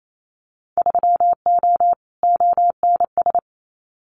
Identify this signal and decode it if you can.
Morse code, heard as a keyed tone, about 28 wpm, 700 Hz. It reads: 3O ONS
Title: Winter Field Day Exchanges Encore 28wpm